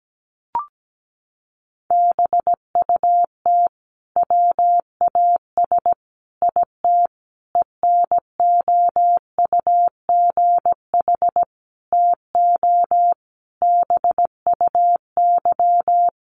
Morse code, heard as a keyed tone, about 17 wpm, 700 Hz. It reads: BUT WAS IT ENOUGH TO BUY
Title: Sentences from Top 1400 Words 17wpm